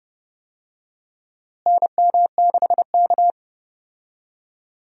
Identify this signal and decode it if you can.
Morse code, heard as a keyed tone, about 30 wpm, 700 Hz. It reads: NM6K
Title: USA Call Signs - 4 Characters - Repeated 2x - 30wpm